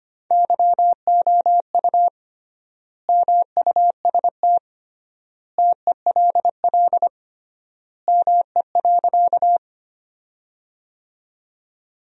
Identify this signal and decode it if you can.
Morse code, heard as a keyed tone, about 25 wpm, 700 Hz. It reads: YOU MUST TELL ME.
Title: 3x Word-Spacing - Sentences from Top 200 Words - 25wpm